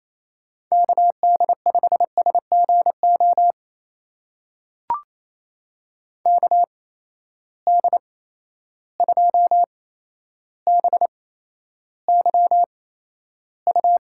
Morse code, heard as a keyed tone, about 28 wpm, spaced as for 12 wpm, 700 Hz. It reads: KD5SGO KD2BYU
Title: USA Call Signs - 6 Characters - 8x Character Spacing - 28wpm